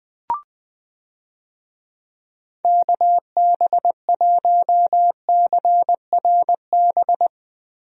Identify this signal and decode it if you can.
Morse code, heard as a keyed tone, about 20 wpm, 700 Hz. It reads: KB1CRB
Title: USA Call Signs - 6 Characters - Repeated 2x - 20wpm